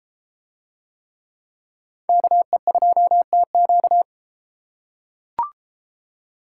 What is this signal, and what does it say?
Morse code, heard as a keyed tone, about 33 wpm, 700 Hz. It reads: KE2TQ